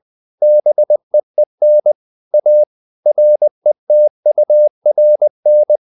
Morse code, heard as a keyed tone, about 20 wpm, 600 Hz. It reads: BEEN A RETURN
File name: the old man and the sea part 2 20wpm